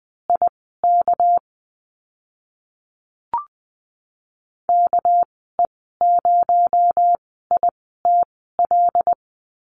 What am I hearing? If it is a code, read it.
Morse code, heard as a keyed tone, about 20 wpm, spaced as for 15 wpm, 700 Hz. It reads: IK KE0ITL
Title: 219 Lesson 35g - Number 0 - Call Signs 20wpm